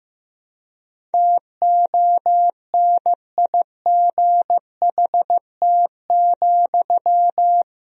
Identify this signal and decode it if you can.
Morse code, heard as a keyed tone, about 15 wpm, 700 Hz. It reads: TONIGHT,
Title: Sentences from Top 1000 Words - Review Part 4 - 15wpm